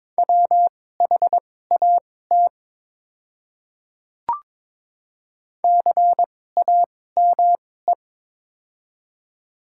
Morse code, heard as a keyed tone, about 22 wpm, spaced as for 16 wpm, 700 Hz. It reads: WHAT CAME
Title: 2x Character Spacing - Top 200 Words - 22wpm